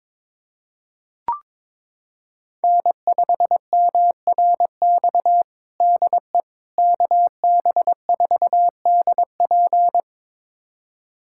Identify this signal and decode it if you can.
Morse code, heard as a keyed tone, about 22 wpm, 700 Hz. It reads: N5MRX DE KB4DP
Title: USA Callsign DE Callsign - 5 Characters - 22wpm